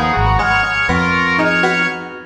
There is no code.